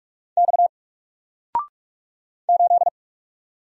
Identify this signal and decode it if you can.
Morse code, heard as a keyed tone, about 46 wpm, 700 Hz. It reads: X 8
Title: Single Letter-Number 45wpm